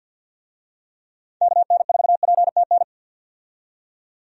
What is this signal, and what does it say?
Morse code, heard as a keyed tone, about 50 wpm, 700 Hz. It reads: KN4PTN
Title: USA Call Signs - 6 Characters - 50wpm